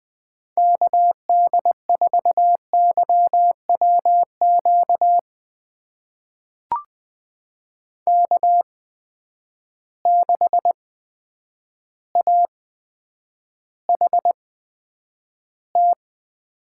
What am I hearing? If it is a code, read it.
Morse code, heard as a keyed tone, about 20 wpm, spaced as for 9 wpm, 700 Hz. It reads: KD4YWQ K6AHT
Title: USA Call Signs - 8x Character Spacing - 20wpm